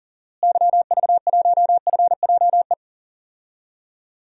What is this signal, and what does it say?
Morse code, heard as a keyed tone, about 40 wpm, 700 Hz. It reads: YV1FJE
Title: DXCC-100 Call Signs - 6 Characters - Repeated 3x - 40wpm